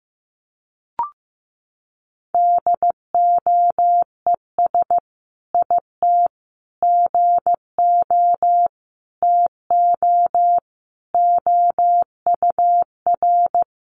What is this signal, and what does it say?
Morse code, heard as a keyed tone, about 15 wpm, 700 Hz. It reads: DOES IT GO TO OUR
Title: Sentences from Top 1500 Words 15wpm